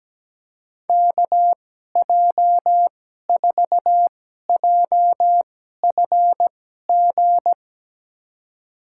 Morse code, heard as a keyed tone, about 17 wpm, spaced as for 12 wpm, 700 Hz. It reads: KJ4JFG
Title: USA Call Signs - 6 Characters - 2x Character Spacing - 17wpm